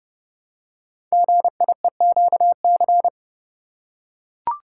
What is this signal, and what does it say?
Morse code, heard as a keyed tone, about 28 wpm, 700 Hz. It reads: GIEQC
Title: Code Groups - 5 Letter:Numbers - 30wpm